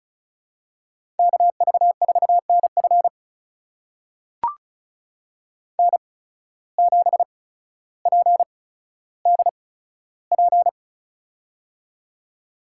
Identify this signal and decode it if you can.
Morse code, heard as a keyed tone, about 36 wpm, spaced as for 15 wpm, 700 Hz. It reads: KV4NF N7PDP
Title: USA Call Signs - 5 Characters - 8x Character Spacing - 35wpm